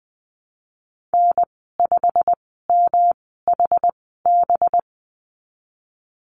Morse code, heard as a keyed tone, about 20 wpm, spaced as for 15 wpm, 700 Hz. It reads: N5MHB
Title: 193 Lesson 31g - Letter v - Call Signs 20wpm